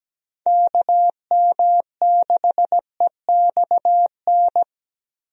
Morse code, heard as a keyed tone, about 17 wpm, 700 Hz. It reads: KM6EXN
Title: USA Call Signs - 6 Characters - Repeated 2x - 17wpm